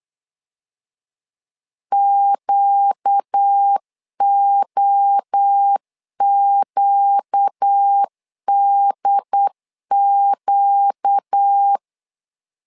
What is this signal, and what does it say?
Morse code, heard as a keyed tone, about 8 wpm, 800 Hz. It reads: QOQDQ
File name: LEZIONI / 4 Q 8wpm 2732